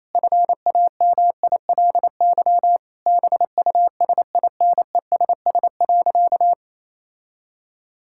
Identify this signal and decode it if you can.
Morse code, heard as a keyed tone, about 28 wpm, 700 Hz. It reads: FAMILY BUSINESS.
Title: Sentences from Top 1000 Words - Review Part 5 - 28wpm